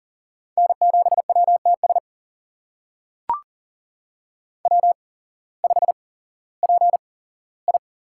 Morse code, heard as a keyed tone, about 39 wpm, spaced as for 19 wpm, 700 Hz. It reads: N7WTS W5PI
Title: USA Call Signs - 8x Character Spacing - 40wpm